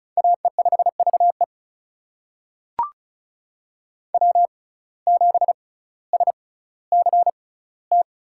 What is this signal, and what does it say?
Morse code, heard as a keyed tone, about 36 wpm, spaced as for 18 wpm, 700 Hz. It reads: AE5VE W7SCT
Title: USA Call Signs - 5 Characters - 6x Character Spacing - 35wpm